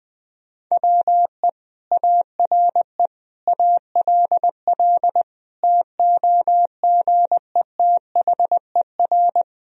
Morse code, heard as a keyed tone, about 20 wpm, 700 Hz. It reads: WE ARE ALL TOGETHER
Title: Sentences from Top 1000 Words - Review Part 2 - 20wpm